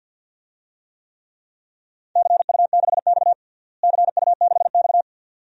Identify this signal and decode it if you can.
Morse code, heard as a keyed tone, about 52 wpm, 700 Hz. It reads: CU6X CU6X
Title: DXCC-100 Call Signs - 4 Characters - Repeated 2x - 50wpm